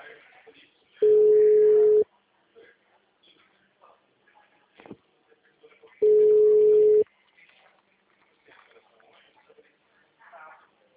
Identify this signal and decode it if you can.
Ringback tone (Europe, 425 Hz), with speech.